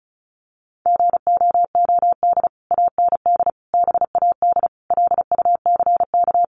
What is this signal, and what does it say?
Morse code, heard as a keyed tone, about 34 wpm, 700 Hz. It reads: GOOD AND BAD LUCK
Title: Sentences from Top 1500 Words 35wpm